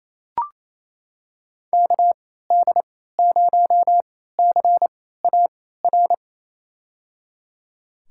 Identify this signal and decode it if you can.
Morse code, heard as a keyed tone, about 28 wpm, spaced as for 16 wpm, 700 Hz. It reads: KD0CAR
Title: USA Call Signs - 6 Characters - 3x Character Spacing - 28wpm